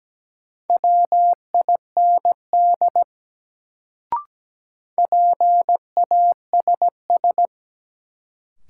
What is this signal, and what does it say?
Morse code, heard as a keyed tone, about 17 wpm, 700 Hz. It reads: WIND PASS
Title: Sentences from Top 400 Words - New Word Drill - 17wpm